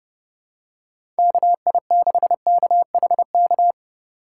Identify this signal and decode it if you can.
Morse code, heard as a keyed tone, about 30 wpm, 700 Hz. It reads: KI6KHK
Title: USA Call Signs - 6 Characters - 6x Character Spacing - 30wpm